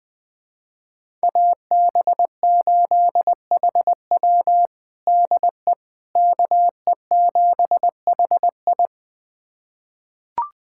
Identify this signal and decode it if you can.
Morse code, heard as a keyed tone, about 21 wpm, 700 Hz. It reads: AB8HW DE KE7HI
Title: USA Callsign DE Callsign - 5 Characters - 20wpm